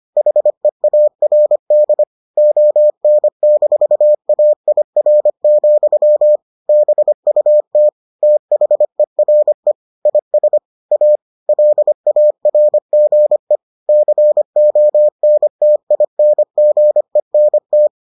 Morse code, heard as a keyed tone, about 25 wpm, 600 Hz. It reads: HEARD ON-AIR, BUT THERE IS A LARGE CONTINGENT